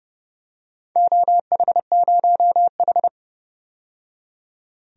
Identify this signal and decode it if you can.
Morse code, heard as a keyed tone, about 30 wpm, 700 Hz. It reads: OH0H
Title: DXCC-100 Call Signs - 4 Characters - 30wpm